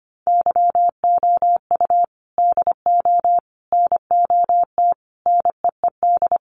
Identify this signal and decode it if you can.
Morse code, heard as a keyed tone, about 25 wpm, 700 Hz. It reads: YOU DO NOT NEED
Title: Sets of 5 Words Encore 25wpm